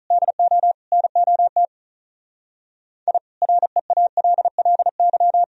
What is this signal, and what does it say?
Morse code, heard as a keyed tone, about 45 wpm, 700 Hz. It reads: DO NOT I REALLY